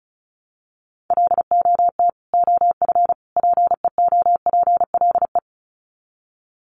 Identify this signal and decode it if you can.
Morse code, heard as a keyed tone, about 34 wpm, 700 Hz. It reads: LOT OF PEOPLE
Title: Sets of 3 Words 35wpm